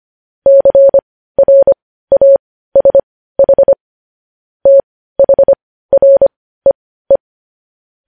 Morse code, heard as a keyed tone, about 25 wpm, spaced as for 15 wpm, 550 Hz. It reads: CRASH THREE
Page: news headlines in morse code at 15 wpm